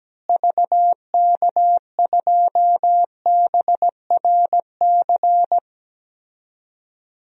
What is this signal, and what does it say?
Morse code, heard as a keyed tone, about 17 wpm, 700 Hz. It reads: VK2BRC
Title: DXCC-100 Call Signs - 6 Characters - Repeated 3x - 17wpm